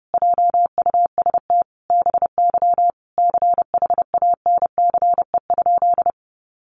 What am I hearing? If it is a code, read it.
Morse code, heard as a keyed tone, about 30 wpm, 700 Hz. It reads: JUST BY CHANCE?
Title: Sentences from Top 800 Words 30wpm